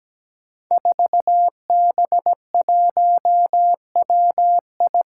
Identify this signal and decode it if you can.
Morse code, heard as a keyed tone, about 17 wpm, 700 Hz. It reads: 4B1WI
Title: Five Challenging Letter-Number Combinations 17wpm